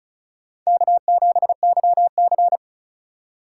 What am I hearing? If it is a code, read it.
Morse code, heard as a keyed tone, about 36 wpm, 700 Hz. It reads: K7YC
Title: USA Call Signs - 3x Character Spacing - 35wpm